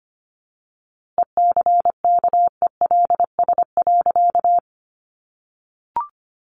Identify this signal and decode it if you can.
Morse code, heard as a keyed tone, about 25 wpm, 700 Hz. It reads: ECKELS.
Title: A Sound Of Thunder 25wpm